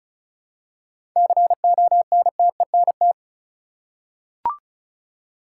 Morse code, heard as a keyed tone, about 34 wpm, 700 Hz. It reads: CONTENT
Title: Seven Letter Words - 35wpm